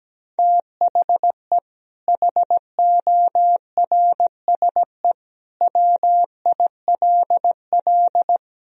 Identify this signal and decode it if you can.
Morse code, heard as a keyed tone, about 17 wpm, 700 Hz. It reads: THE HORSE WILL
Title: Sentences from Top 1000 Words - Review Part 2 - 17wpm